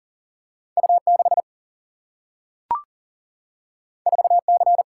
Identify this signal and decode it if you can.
Morse code, heard as a keyed tone, about 40 wpm, 700 Hz. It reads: U6 4C
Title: Single-Letter Single-Number 40wpm